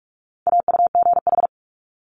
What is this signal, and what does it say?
Morse code, heard as a keyed tone, about 46 wpm, 700 Hz. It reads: AUGH
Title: Multi-letter Phonemes Quick Fire 45wpm